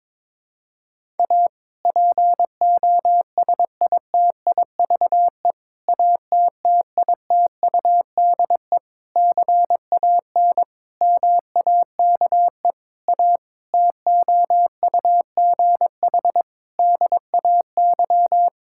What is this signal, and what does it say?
Morse code, heard as a keyed tone, about 22 wpm, 700 Hz. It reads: A POSITIVE ATTITUDE CAN MAKE A TOUGH DAY
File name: Difficult Sentences with Hints - 22wpm